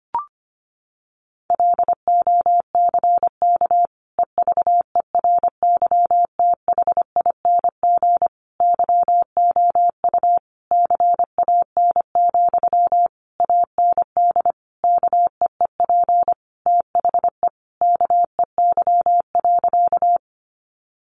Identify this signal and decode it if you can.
Morse code, heard as a keyed tone, about 25 wpm, 700 Hz. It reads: LOCK EVERYTHING YOU CAN, AND KEEP THE KEY.
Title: Sentences from Top 1300 Words 25wpm